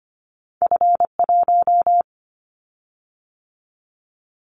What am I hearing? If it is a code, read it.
Morse code, heard as a keyed tone, about 25 wpm, 700 Hz. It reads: F1
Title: Callsign Prefixes for Top 25 DXCC - 25wpm